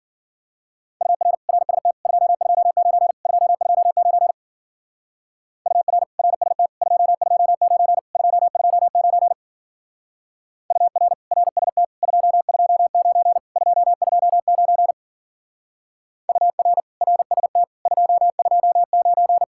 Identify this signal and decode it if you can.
Morse code, heard as a keyed tone, about 48 wpm, 700 Hz. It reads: UR RST 229 229 UR RST 229 229 UR RST 229 229 UR RST 229
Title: Speed-Racing Common QSO Phrases - 40wpm